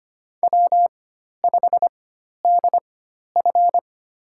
Morse code, heard as a keyed tone, about 25 wpm, spaced as for 12 wpm, 690 Hz. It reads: W5DF
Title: USA Call Signs - 4 Characters - 4x Character Spacing - 25wpm